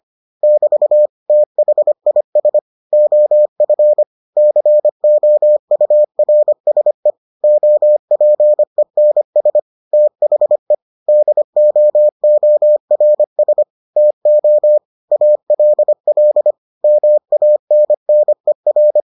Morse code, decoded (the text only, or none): <BT> THIS OF COURSE OPENS THE DOORS TO ALL MANNER